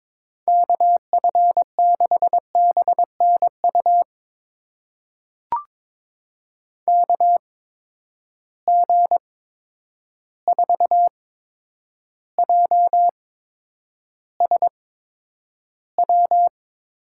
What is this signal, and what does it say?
Morse code, heard as a keyed tone, about 22 wpm, spaced as for 9 wpm, 700 Hz. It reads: KF6BNU KG4JSW
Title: USA Call Signs - 6 Characters - 8x Character Spacing - 22wpm